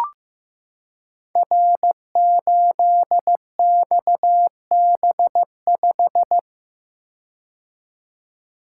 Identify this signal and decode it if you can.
Morse code, heard as a keyed tone, about 15 wpm, 700 Hz. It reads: R8XB5
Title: Code Groups - 5 Letter:Numbers - 15wpm